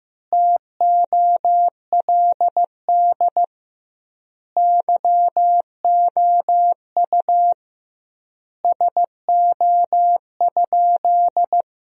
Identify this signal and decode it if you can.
Morse code, heard as a keyed tone, about 15 wpm, 700 Hz. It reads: TOLD YOU SO?
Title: 2x Word-Spacing - Sentences from Top 400 Words - 15wpm